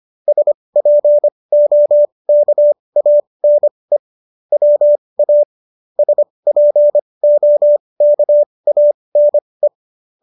Morse code, heard as a keyed tone, about 25 wpm, spaced as for 20 wpm, 590 Hz. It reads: SPOKANE WA SPOKANE